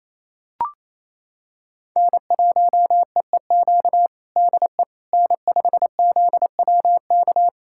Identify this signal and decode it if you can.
Morse code, heard as a keyed tone, about 28 wpm, 700 Hz. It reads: N1EEQ DE N5ZWK
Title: USA Callsign DE Callsign - 5 Characters - 28wpm